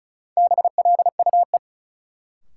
Morse code, heard as a keyed tone, about 36 wpm, 700 Hz. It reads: BLUE
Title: Sentences from Top 1200 Words - New Word Drill Alt - 35wpm